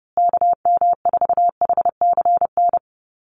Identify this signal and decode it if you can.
Morse code, heard as a keyed tone, about 30 wpm, 700 Hz. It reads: KM4HCN
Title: USA Call Signs Repeated 2x 30wpm